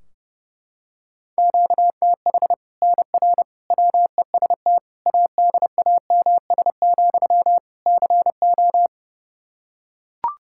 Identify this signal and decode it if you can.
Morse code, heard as a keyed tone, about 30 wpm, 700 Hz. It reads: QTH NR WEST ADAMS, CO
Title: Practice QSO - Exchange 1 - Beginning - 30wpm